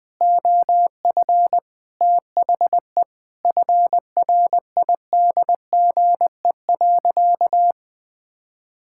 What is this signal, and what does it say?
Morse code, heard as a keyed tone, about 20 wpm, 700 Hz. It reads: OF THE FRIDGE.
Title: Sentences with 4-Word Phrases - 20wpm